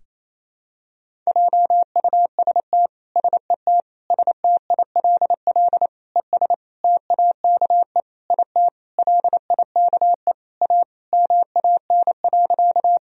Morse code, heard as a keyed tone, about 28 wpm, 700 Hz. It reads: JUST SET STILL ES TAKE IT LIKE A MAN.
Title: Sentences from Top 300 Words and Abbreviations 28wpm